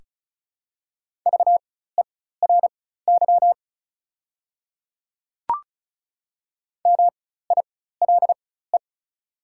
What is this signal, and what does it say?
Morse code, heard as a keyed tone, about 35 wpm, spaced as for 16 wpm, 690 Hz. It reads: VERY MILE
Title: 4x Character Spacing - Top 500 Words - 35wpm